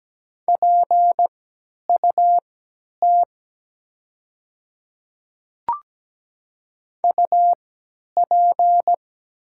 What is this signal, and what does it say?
Morse code, heard as a keyed tone, about 17 wpm, spaced as for 10 wpm, 700 Hz. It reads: PUT UP
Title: 3x Character Spacing - Top 200 Words - 17wpm